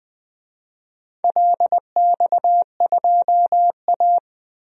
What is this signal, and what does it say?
Morse code, heard as a keyed tone, about 20 wpm, 700 Hz. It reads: LX2A
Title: DXCC-100 Call Signs - 4 Characters - 20wpm